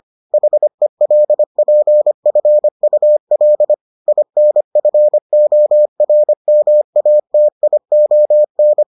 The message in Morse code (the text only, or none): HELPFUL INFORMATION